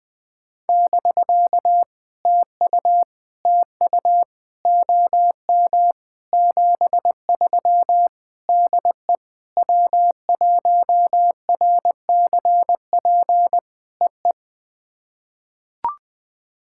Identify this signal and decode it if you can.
Morse code, heard as a keyed tone, about 20 wpm, 700 Hz. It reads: <BK> TU TU OM 73 DE W1RCP EE
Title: POTA QSO Practice - 20wpm